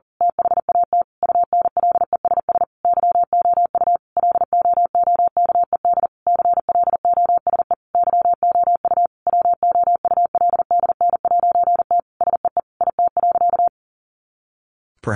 Morse code, heard as a keyed tone, about 37 wpm, 700 Hz. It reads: THAT UNLESS YOU LOOKED CLOSE YOU WOULDN'T SEE IT.
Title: A Sound Of Thunder 40wpm